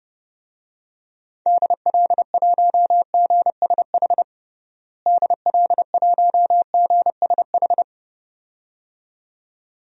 Morse code, heard as a keyed tone, about 30 wpm, 700 Hz. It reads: DL1GSH DL1GSH
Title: DXCC-100 Call Signs - 6 Characters - Repeated 2x - 30wpm